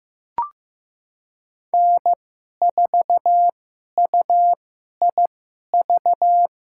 Morse code, heard as a keyed tone, about 15 wpm, spaced as for 11 wpm, 700 Hz. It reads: N4UIV
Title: USA Call Signs - 5 Characters - 2x Character Spacing - 15wpm